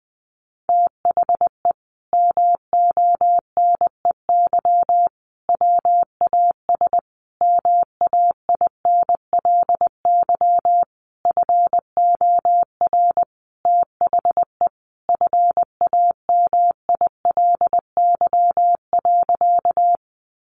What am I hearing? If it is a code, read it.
Morse code, heard as a keyed tone, about 20 wpm, 700 Hz. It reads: THE MONEY WAS MAINLY FOR THE FAMILY.
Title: Sentences from Top 1400 Words 20wpm